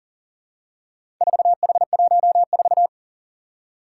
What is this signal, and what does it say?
Morse code, heard as a keyed tone, about 37 wpm, 700 Hz. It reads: 4H14
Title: Four Challenging Letter-Number Combinations 40wpm